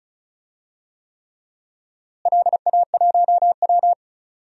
Morse code, heard as a keyed tone, about 34 wpm, 700 Hz. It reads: LA1W
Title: DXCC-100 Call Signs - 4 Characters - Repeated 3x - 35wpm